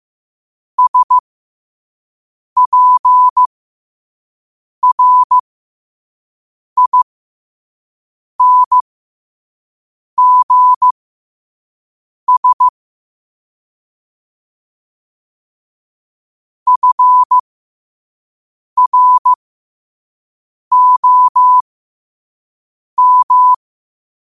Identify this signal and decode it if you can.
Morse code, heard as a keyed tone, about 15 wpm, spaced as for 5 wpm, 1000 Hz. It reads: SPRINGS FROM
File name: qotdfiles / 5wpm